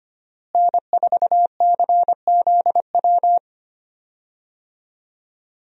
Morse code, heard as a keyed tone, about 25 wpm, 700 Hz. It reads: N4CZW
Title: USA Call Signs - 5 Characters - Repeated 2x - 25wpm